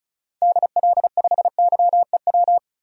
Morse code, heard as a keyed tone, about 36 wpm, 700 Hz. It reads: DL5YEW
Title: DXCC-100 Call Signs - 6 Characters - 35wpm